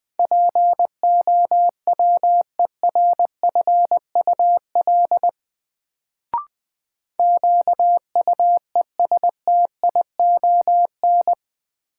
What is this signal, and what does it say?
Morse code, heard as a keyed tone, about 21 wpm, 700 Hz. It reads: POWERFUL QUESTION
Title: Sets of 5 Words Encore - New Word Drill - 20wpm